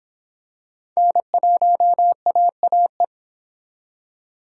Morse code, heard as a keyed tone, about 25 wpm, 700 Hz. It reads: N1AAE